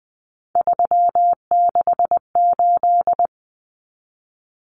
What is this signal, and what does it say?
Morse code, heard as a keyed tone, about 20 wpm, 700 Hz. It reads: 368
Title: Numbers - 3-Digit 20wpm